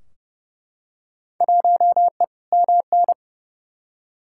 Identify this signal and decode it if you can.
Morse code, heard as a keyed tone, about 30 wpm, 700 Hz. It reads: 1E MN
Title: Field Day Exchanges - 30wpm